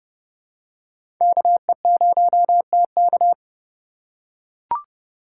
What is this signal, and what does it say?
Morse code, heard as a keyed tone, about 30 wpm, 700 Hz. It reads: KE0TK
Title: USA Call Signs - 5 Characters - 4x Character Spacing - 30wpm